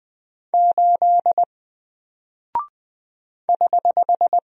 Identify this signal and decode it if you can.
Morse code, heard as a keyed tone, about 20 wpm, 700 Hz. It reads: 8 <HH>
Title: Single Character (Letter:Number:Punctuation:Prosign) 20wpm